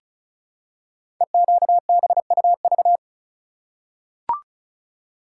Morse code, heard as a keyed tone, about 36 wpm, 700 Hz. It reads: EQBUV
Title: Code Groups - 5 Letters - 35wpm